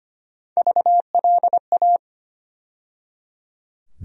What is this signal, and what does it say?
Morse code, heard as a keyed tone, about 25 wpm, 700 Hz. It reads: VLA
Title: USA Call Sign Postfixes - 3 Letters 25wpm